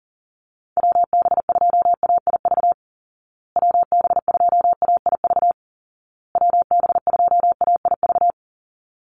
Morse code, heard as a keyed tone, about 37 wpm, 700 Hz. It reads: WB2AIV WB2AIV WB2AIV
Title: Call Signs Repeated 2x - 40wpm